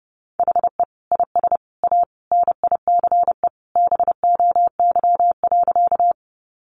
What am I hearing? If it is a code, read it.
Morse code, heard as a keyed tone, about 30 wpm, 700 Hz. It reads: HE IS A NICE BOY.
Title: Sentences from Top 1100 Words 30wpm